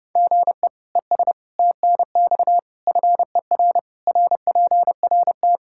Morse code, heard as a keyed tone, about 30 wpm, 700 Hz. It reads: GE ES TNX FER RPRT
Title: K1USN SST QSO Practice 30wpm